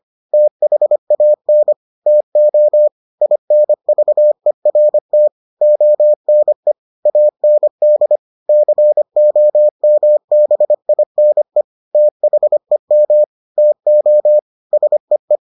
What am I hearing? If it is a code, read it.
Morse code, heard as a keyed tone, about 25 wpm, 600 Hz. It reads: THAN TO INVERT ONE AND COMBINE THEM TO SEE